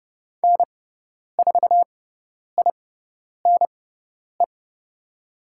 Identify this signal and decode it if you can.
Morse code, heard as a keyed tone, about 29 wpm, spaced as for 10 wpm, 700 Hz. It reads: N4INE
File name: Lesson 17g - Letter u - Call Signs 30wpm